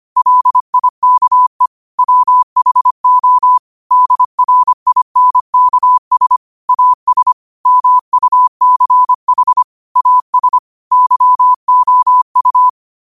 Morse code, heard as a keyed tone, about 25 wpm, 1000 Hz. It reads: LIKE WHO DRINKS AS MUCH AS YOU